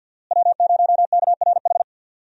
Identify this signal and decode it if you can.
Morse code, heard as a keyed tone, about 52 wpm, 700 Hz. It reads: W0KRH
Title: USA Call Signs - 5 Characters - Repeated 2x - 50wpm